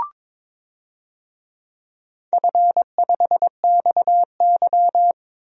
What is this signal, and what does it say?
Morse code, heard as a keyed tone, about 22 wpm, 700 Hz. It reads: F5XY
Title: DXCC-100 Call Signs - 4 Characters - 22wpm